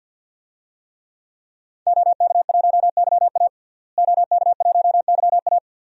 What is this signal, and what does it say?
Morse code, heard as a keyed tone, about 49 wpm, 700 Hz. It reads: OK1YA OK1YA
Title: DXCC-100 Call Signs - 5 Characters - Repeated 2x - 50wpm